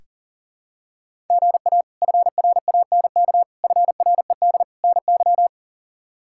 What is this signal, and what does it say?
Morse code, heard as a keyed tone, about 37 wpm, 700 Hz. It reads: GA FRANK FRED NY
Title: K1USN SST QSO Practice - 40wpm